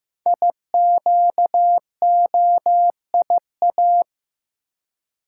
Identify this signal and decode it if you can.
Morse code, heard as a keyed tone, about 15 wpm, 700 Hz. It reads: IQOIA
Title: Code Groups - 5 Letters - 15wpm